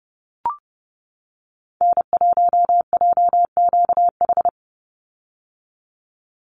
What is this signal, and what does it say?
Morse code, heard as a keyed tone, about 30 wpm, 700 Hz. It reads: N1JQH
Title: USA Call Signs 30wpm